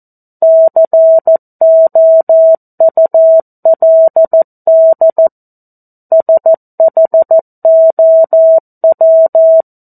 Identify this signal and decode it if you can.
Morse code, heard as a keyed tone, about 14 wpm, 650 Hz. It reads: COULD SHOW
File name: 02-Chapter-02 14 WPM